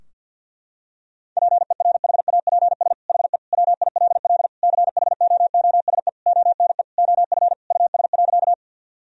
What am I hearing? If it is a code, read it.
Morse code, heard as a keyed tone, about 52 wpm, 700 Hz. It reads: PERHAPS HE WILL CHOOSE ONE OF US.
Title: Sentences from Top 1000 Words - Review Part 4 - 50wpm